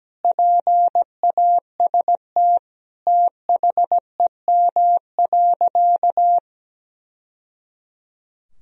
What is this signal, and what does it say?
Morse code, heard as a keyed tone, about 17 wpm, 700 Hz. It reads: PAST THEM.
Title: Sentences from Top 1000 Words - Review Part 3 - 17wpm